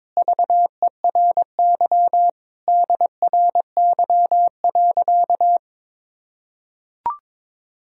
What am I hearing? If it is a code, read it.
Morse code, heard as a keyed tone, about 22 wpm, 700 Hz. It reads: VERY DRY.
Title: Sentences from Top 1000 Words - Review Part 3 - 22wpm